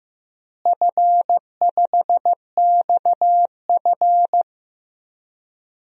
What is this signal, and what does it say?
Morse code, heard as a keyed tone, about 15 wpm, 700 Hz. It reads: F5XF
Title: DXCC-100 Call Signs - 4 Characters - Repeated 3x - 15wpm